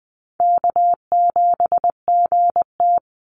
Morse code, heard as a keyed tone, about 20 wpm, 700 Hz. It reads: K7GT
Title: Call Signs - 20wpm